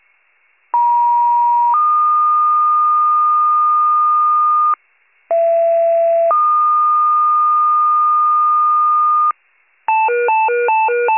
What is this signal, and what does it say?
Two-tone page, heard as a one-second tone, then a three-second tone, about 950 then 1220 Hz, about 670 then 1220 Hz. It is heard from 0.7 to 9.3 s.